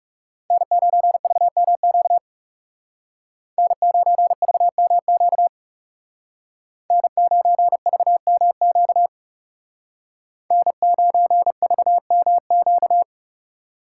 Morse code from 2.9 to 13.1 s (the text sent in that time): N9VMQ N9VMQ N9VMQ